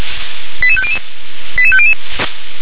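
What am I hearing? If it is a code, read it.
Five-tone selective call (ZVEI), heard as quick runs of five tones, tones of about 70 ms. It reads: 88500 88500
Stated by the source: Five-tone ZVEI selective call received by SDR.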